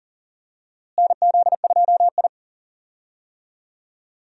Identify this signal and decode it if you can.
Morse code, heard as a keyed tone, about 40 wpm, 700 Hz. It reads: NZ2I
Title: USA Call Signs - 4 Characters - Repeated 2x - 40wpm